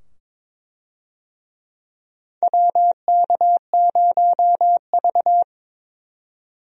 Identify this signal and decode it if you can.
Morse code, heard as a keyed tone, about 22 wpm, 700 Hz. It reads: WK0V
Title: USA Call Signs - 4 Characters - 22wpm